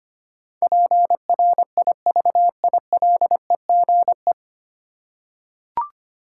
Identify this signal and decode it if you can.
Morse code, heard as a keyed tone, about 25 wpm, 700 Hz. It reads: PRIVILEGE
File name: Nine Letter Words - 25wpm